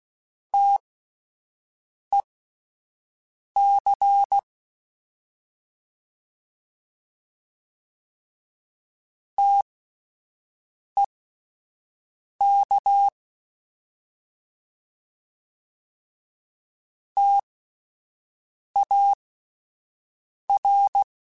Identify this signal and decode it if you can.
Morse code, heard as a keyed tone, about 16 wpm, spaced as for 5 wpm, 800 Hz. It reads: TEC TEK TAR